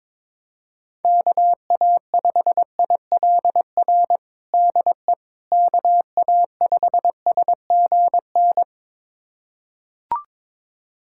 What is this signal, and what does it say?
Morse code, heard as a keyed tone, about 22 wpm, 700 Hz. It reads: KA5ILR DE KA5SGN
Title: USA Callsign DE Callsign - 6 Characters - 22wpm